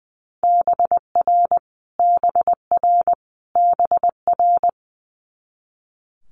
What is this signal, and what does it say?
Morse code, heard as a keyed tone, about 20 wpm, 700 Hz. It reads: BR BR BR
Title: Multi-letter Phonemes Repeated 3x 20wpm